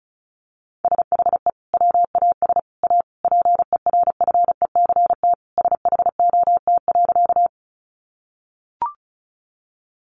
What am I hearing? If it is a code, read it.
Morse code, heard as a keyed tone, about 34 wpm, 700 Hz. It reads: SHE WAS A PERFECT SHOT.
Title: Sentences from Top 1100 Words 35wpm